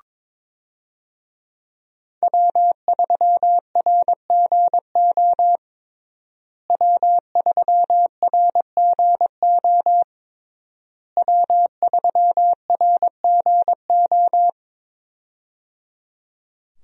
Morse code, heard as a keyed tone, about 22 wpm, 700 Hz. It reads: W3RGO W3RGO W3RGO
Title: USA Call Signs - 5 Characters - Repeated 3x - 22wpm